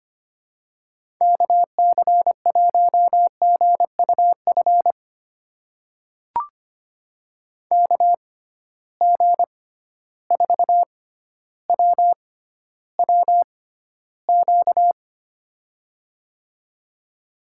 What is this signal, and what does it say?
Morse code, heard as a keyed tone, about 25 wpm, spaced as for 13 wpm, 700 Hz. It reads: KC1GUF KG4WWQ
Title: USA Call Signs - 6 Characters - 6x Character Spacing - 25wpm